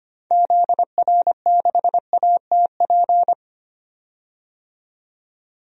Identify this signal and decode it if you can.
Morse code, heard as a keyed tone, about 25 wpm, 700 Hz. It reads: ZR6ATP
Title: DXCC-100 Call Signs - 6 Characters - 25wpm